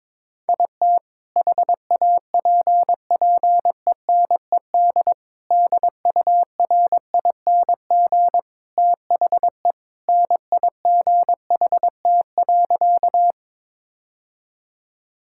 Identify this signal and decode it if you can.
Morse code, heard as a keyed tone, about 22 wpm, 700 Hz. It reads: IT HAPPENED DURING THE NIGHT.
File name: practice / 22wpm / Sentences with 2-Word Phrases - 22wpm